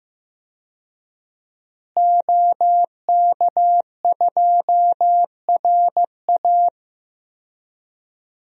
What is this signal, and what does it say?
Morse code, heard as a keyed tone, about 15 wpm, 700 Hz. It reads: OK2RA
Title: DXCC-100 Call Signs - 5 Characters - Repeated 2x - 15wpm